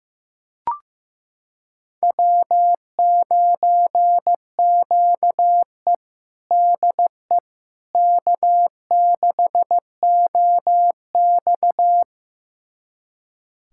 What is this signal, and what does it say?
Morse code, heard as a keyed tone, about 15 wpm, 700 Hz. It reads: W9QE DE K6OX
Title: USA Callsign DE Callsign - 4 Characters - 15wpm